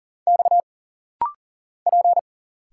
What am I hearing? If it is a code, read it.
Morse code, heard as a keyed tone, about 40 wpm, 700 Hz. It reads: X P
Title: Single Letters - Rapid-Fire 40wpm